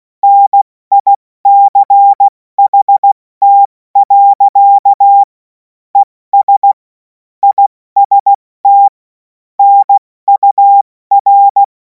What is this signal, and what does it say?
Morse code, heard as a keyed tone, about 16 wpm, 800 Hz. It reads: NICHT. ES IST NUR
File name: Morsekurs / Witze / Witz 02. Juni 2026 70 F80